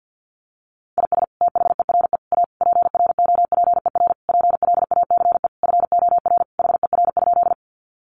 Morse code, heard as a keyed tone, about 50 wpm, 700 Hz. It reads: IS THERE A PROPER PLACE FOR HER?
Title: Sentences from Top 800 Words 50wpm